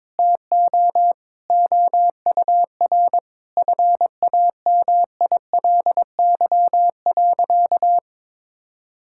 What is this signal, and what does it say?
Morse code, heard as a keyed tone, about 22 wpm, 700 Hz. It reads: TO OUR FAMILY.
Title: Sentences from Top 1000 Words - Review Part 6 - 22wpm